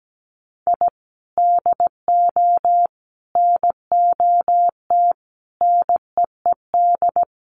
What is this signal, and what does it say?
Morse code, heard as a keyed tone, about 17 wpm, 700 Hz. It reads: I DO NOT NEED
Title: Sets of 5 Words 17wpm